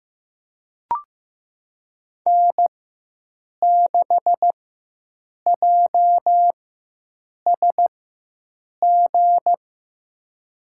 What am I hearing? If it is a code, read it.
Morse code, heard as a keyed tone, about 15 wpm, spaced as for 7 wpm, 700 Hz. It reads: N6JSG
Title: USA Call Signs - 5 Characters - 4x Character Spacing - 15wpm